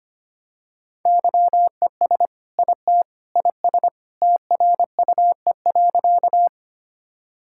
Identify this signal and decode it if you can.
Morse code, heard as a keyed tone, about 25 wpm, 700 Hz. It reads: YES IT IS TRUE.
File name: practice / 25wpm / Sentences with 2-Word Phrases - 25wpm